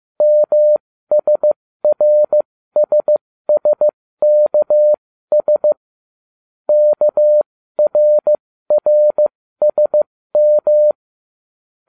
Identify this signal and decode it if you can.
Morse code, heard as a keyed tone, about 15 wpm, spaced as for 13 wpm, 600 Hz. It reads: MSRSSKS KRRSM